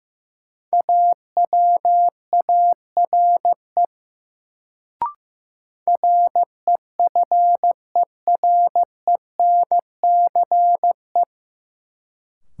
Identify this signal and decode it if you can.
Morse code, heard as a keyed tone, about 15 wpm, 700 Hz. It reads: AWARE REFERENCE
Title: Sentences from Top 1200 Words - New Word Drill - 15wpm